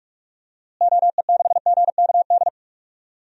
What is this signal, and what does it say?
Morse code, heard as a keyed tone, about 42 wpm, 700 Hz. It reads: OE6GKD
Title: DXCC-100 Call Signs - 6 Characters - Repeated 2x - 45wpm